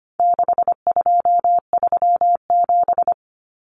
Morse code, heard as a keyed tone, about 25 wpm, 700 Hz. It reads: 6237
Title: Numbers - 4-Digit 25wpm